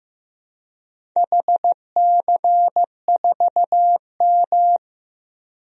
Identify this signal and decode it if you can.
Morse code, heard as a keyed tone, about 15 wpm, 700 Hz. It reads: HC4M
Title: DXCC-100 Call Signs - 4 Characters - 15wpm